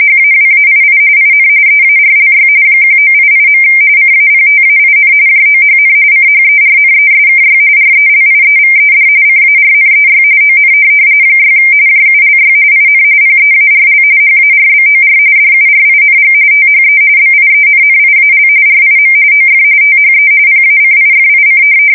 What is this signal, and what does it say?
RTTY, about 45.45 baud. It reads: HAWAII IN EARLY 1988. CAPTAIN JOHN CHERNESKY RELIEVED CARNEY ON 6 JULY IN PEARL HARBOR DURING THE BIENNIAL RIM OF THE PACIFIC
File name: fsk170 / USS-Missouri